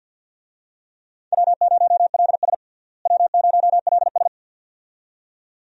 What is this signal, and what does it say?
Morse code, heard as a keyed tone, about 52 wpm, 700 Hz. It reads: W0LS W0LS
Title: USA Call Signs - 4 Characters - Repeated 2x - 50wpm